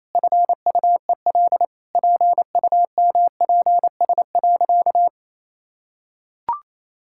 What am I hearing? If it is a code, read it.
Morse code, heard as a keyed tone, about 28 wpm, 700 Hz. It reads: FUEL PUMPS.
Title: Pangram Challenge v2 - 28wpm